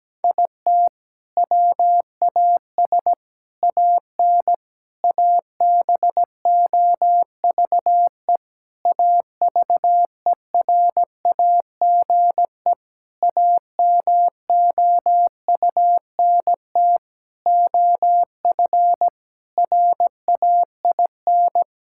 Morse code, heard as a keyed tone, about 17 wpm, 700 Hz. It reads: IT WAS AN ABOVE AVERAGE AMOUNT OF RAIN
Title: Sentences from Top 1000 Words - Review Part 6 - 17wpm